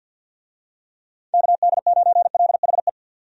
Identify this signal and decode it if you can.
Morse code, heard as a keyed tone, about 52 wpm, 700 Hz. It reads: KD9LHE
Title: USA Call Signs - 6 Characters - Repeated 2x - 50wpm